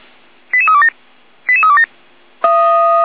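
Five-tone selective call (ZVEI), heard as quick runs of five tones, tones of about 70 ms. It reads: 88327 88327
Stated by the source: ZVEI five-tone selective call received on an SDR.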